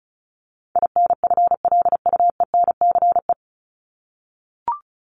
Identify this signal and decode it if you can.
Morse code, heard as a keyed tone, about 34 wpm, 700 Hz. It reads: INFLUENCE
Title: Top 1001-1250 Words 35wpm